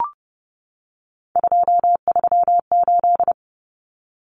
Morse code, heard as a keyed tone, about 30 wpm, 700 Hz. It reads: 238
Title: Numbers - 3-Digit 30wpm